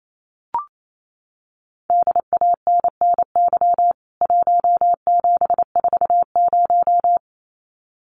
Morse code, heard as a keyed tone, about 28 wpm, 700 Hz. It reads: DANNY 1740
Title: CWT Exchange 28wpm